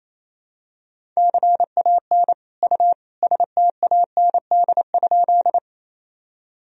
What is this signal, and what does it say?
Morse code, heard as a keyed tone, about 27 wpm, 700 Hz. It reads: CAN U STAND?
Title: Sentences from Top 300 Words and Abbreviations 28wpm